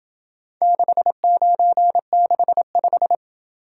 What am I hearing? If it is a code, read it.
Morse code, heard as a keyed tone, about 27 wpm, 700 Hz. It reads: 6965